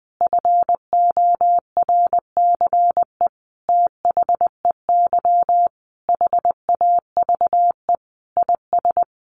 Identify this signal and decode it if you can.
Morse code, heard as a keyed tone, about 20 wpm, 700 Hz. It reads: FORCE THEY HAVE IS
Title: Sentences from Top 1500 Words 20wpm